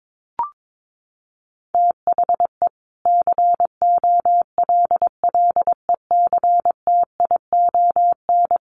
Morse code, heard as a keyed tone, about 22 wpm, 700 Hz. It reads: THE COLLECTION
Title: Sentences from Top 1200 Words 22wpm